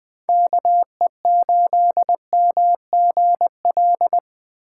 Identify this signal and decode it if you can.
Morse code, heard as a keyed tone, about 20 wpm, 700 Hz. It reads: KE8MGL
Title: USA Call Signs - 6 Characters - 4x Character Spacing - 20wpm